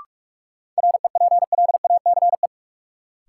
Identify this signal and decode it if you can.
Morse code, heard as a keyed tone, about 46 wpm, 700 Hz. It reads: REPLACE
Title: Sentences from Top 1400 Words - New Word Drill - 45wpm